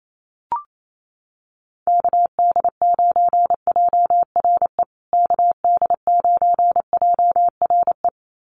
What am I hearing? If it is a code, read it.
Morse code, heard as a keyed tone, about 28 wpm, 700 Hz. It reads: KD9JRE KD9JRE
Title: USA Call Signs Repeated 2x 28wpm